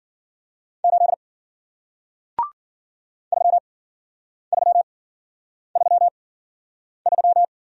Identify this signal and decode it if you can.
Morse code, heard as a keyed tone, about 53 wpm, spaced as for 9 wpm, 700 Hz. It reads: 8 3333